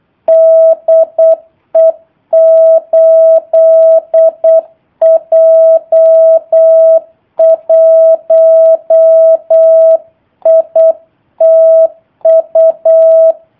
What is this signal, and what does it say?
Morse code, heard as a keyed tone, about 8 wpm, 650 Hz. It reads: DE 8J1ITU
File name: jh8cbh / 2019cw / sp03- w